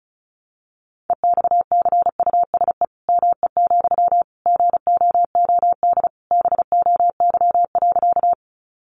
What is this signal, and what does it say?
Morse code, heard as a keyed tone, about 34 wpm, 700 Hz. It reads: EXCUSE ME, GOOD BOY.
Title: Sentences from Top 700 Words 35wpm